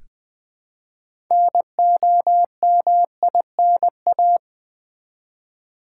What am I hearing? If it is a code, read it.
Morse code, heard as a keyed tone, about 20 wpm, 700 Hz. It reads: NOMINA
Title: Build A Word - 3 Letters At A Time - 8-Letter Words - 20wpm